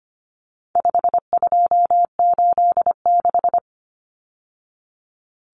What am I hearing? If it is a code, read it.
Morse code, heard as a keyed tone, about 25 wpm, 700 Hz. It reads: 5286
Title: Numbers - 4-Digit 25wpm